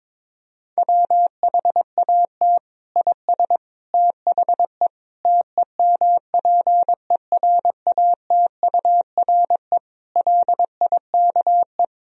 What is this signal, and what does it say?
Morse code, heard as a keyed tone, about 22 wpm, 700 Hz. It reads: WHAT IS THE TEMPERATURE LIKE
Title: Sentences from Top 1000 Words - Review Part 4 - 22wpm